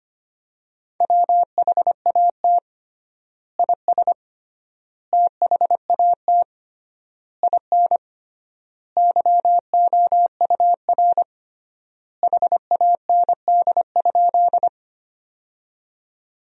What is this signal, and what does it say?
Morse code, heard as a keyed tone, about 25 wpm, 700 Hz. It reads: WHAT IS THAT IN YOUR HAND?
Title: 3x Word-Spacing - Sentences from Top 200 Words - 25wpm